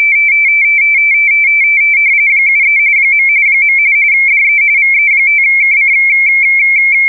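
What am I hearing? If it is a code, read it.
RTTY, about 45.45 baud. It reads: 73 . DE VE5KC
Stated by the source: RTTY test recording for the JE3HHT MMTTY engine used with UcxLog logging software.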